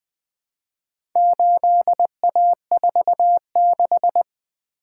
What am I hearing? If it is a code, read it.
Morse code, heard as a keyed tone, about 20 wpm, 700 Hz. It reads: 8A46
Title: Four Letter-Number Combinations - 20wpm